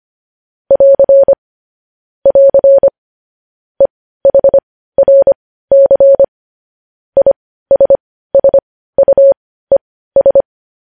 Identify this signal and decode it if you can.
Morse code, heard as a keyed tone, about 25 wpm, spaced as for 15 wpm, 550 Hz. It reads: <AR> <AR> EHRC ISSUES
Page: news-headlines-in-morse-code-at-15-wpm